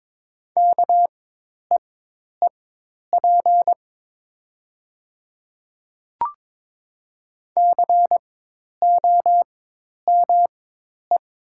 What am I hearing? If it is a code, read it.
Morse code, heard as a keyed tone, about 22 wpm, spaced as for 10 wpm, 700 Hz. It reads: KEEP COME
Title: 4x Character Spacing - Top 300 Words - 22wpm